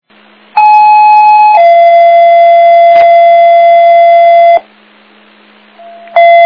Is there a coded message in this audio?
Yes: two-tone page from 0.6 to 4.6 s, heard as a one-second tone, then a three-second tone, about 850 then 680 Hz.